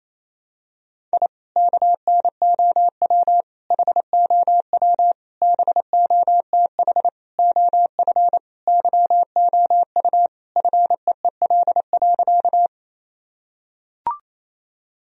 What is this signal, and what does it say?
Morse code, heard as a keyed tone, about 28 wpm, 700 Hz. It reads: I KNOW HOW BOTH OF YOU FEEL.
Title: Sentences from Top 1000 Words - Review Part 1 - 28wpm